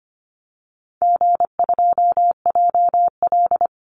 Morse code, heard as a keyed tone, about 24 wpm, 700 Hz. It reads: G2JL
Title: Easy Call Signs Repeated 3x - 25wpm